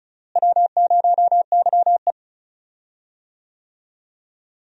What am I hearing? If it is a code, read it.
Morse code, heard as a keyed tone, about 36 wpm, 700 Hz. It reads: W0YE
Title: USA Call Signs - 4 Characters - Repeated 3x - 35wpm